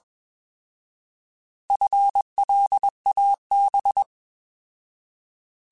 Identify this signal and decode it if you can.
Morse code, heard as a keyed tone, about 20 wpm, 800 Hz. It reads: FLAB